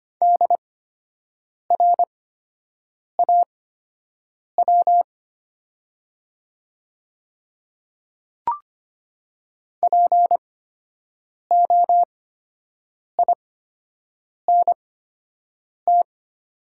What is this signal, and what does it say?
Morse code, heard as a keyed tone, about 25 wpm, spaced as for 7 wpm, 700 Hz. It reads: DRAW POINT
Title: 8x Character Spacing - Top 500 Words - 25wpm